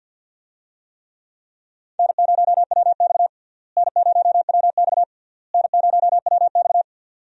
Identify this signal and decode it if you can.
Morse code, heard as a keyed tone, about 53 wpm, 700 Hz. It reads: N0WX N0WX N0WX
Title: USA Call Signs - 4 Characters - Repeated 3x - 50wpm